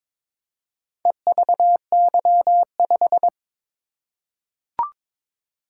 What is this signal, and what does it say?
Morse code, heard as a keyed tone, about 22 wpm, 700 Hz. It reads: EVY5
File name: practice / 22wpm / Four Letter-Number Combinations - 22wpm